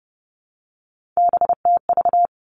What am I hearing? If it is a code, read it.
Morse code, heard as a keyed tone, about 30 wpm, 700 Hz. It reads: BTV
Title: Three Challenging Letter-Number Combinations 30wpm